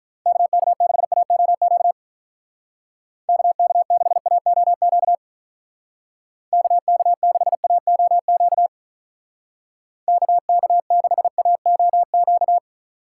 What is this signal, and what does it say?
Morse code, heard as a keyed tone, about 42 wpm, 700 Hz. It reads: KK6AOQ KK6AOQ KK6AOQ KK6AOQ
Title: Speed Racing - USA 2x3 Call Signs 35wpm